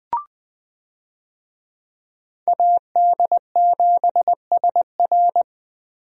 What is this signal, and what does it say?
Morse code, heard as a keyed tone, about 20 wpm, 700 Hz. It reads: AD7SR
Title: USA Call Signs - 5 Characters - Repeated 3x - 20wpm